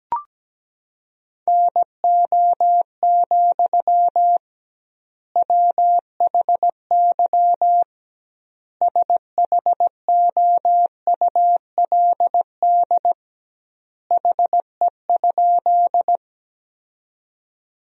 Morse code, heard as a keyed tone, about 17 wpm, 700 Hz. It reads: NO, WHY SHOULD HE?
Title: 2x Word-Spacing - Sentences from Top 300 Words - 17wpm